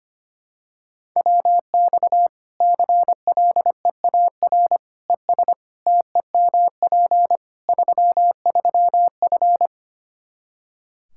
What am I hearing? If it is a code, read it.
Morse code, heard as a keyed tone, about 25 wpm, 700 Hz. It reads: WX CLEAR ES TEMP 33F
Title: Practice QSO - Exchange 2 - Info - 25wpm